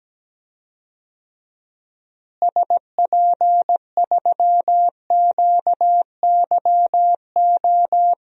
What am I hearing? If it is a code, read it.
Morse code, heard as a keyed tone, about 17 wpm, 700 Hz. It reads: SP3QYO
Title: DXCC-100 Call Signs - 6 Characters - Repeated 3x - 17wpm